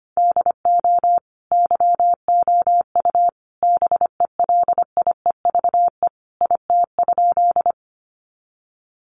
Morse code, heard as a keyed tone, about 25 wpm, 700 Hz. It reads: DO YOU BELIEVE IT?
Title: Sentences from Top 1000 Words 25wpm